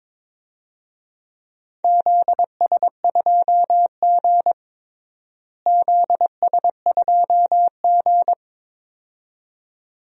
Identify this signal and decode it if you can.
Morse code, heard as a keyed tone, about 21 wpm, 700 Hz. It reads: ZS2G ZS2G
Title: DXCC-100 Call Signs - 4 Characters - Repeated 2x - 22wpm